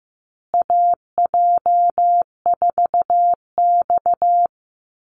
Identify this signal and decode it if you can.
Morse code, heard as a keyed tone, about 15 wpm, 700 Hz. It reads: AJ4X
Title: USA Call Signs Repeated 2x 15wpm